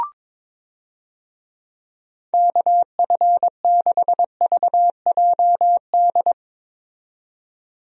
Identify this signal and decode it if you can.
Morse code, heard as a keyed tone, about 22 wpm, 700 Hz. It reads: KF6VJD
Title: USA Call Signs - 6 Characters - 22wpm